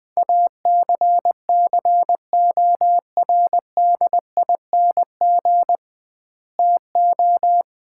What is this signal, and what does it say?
Morse code, heard as a keyed tone, about 20 wpm, 700 Hz. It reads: ACCORDING TO
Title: 2x Word-Spacing - Sets of 2 Words - 20wpm